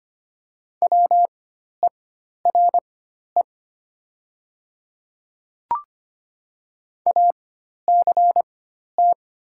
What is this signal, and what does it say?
Morse code, heard as a keyed tone, about 25 wpm, spaced as for 12 wpm, 700 Hz. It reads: WERE ACT
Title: 4x Character Spacing - Top 200 Words - 25wpm